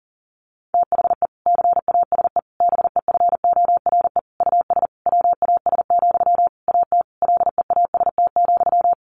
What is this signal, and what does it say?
Morse code, heard as a keyed tone, about 40 wpm, 700 Hz. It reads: THE CASE BEFORE US WAS, AT LEAST,
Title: Sentences from Top 1400 Words 40wpm